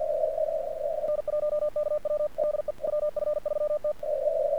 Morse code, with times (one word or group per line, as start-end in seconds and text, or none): 1.1-2.3 N1KW
2.4-2.7 DE
2.9-3.9 WF3T